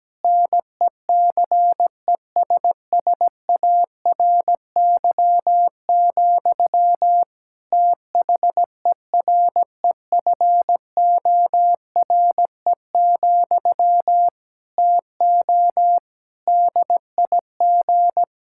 Morse code, heard as a keyed tone, about 17 wpm, 700 Hz. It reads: NECESSARY, THEREFORE, TO DIG
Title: Sentences from Top 1600 Words 17wpm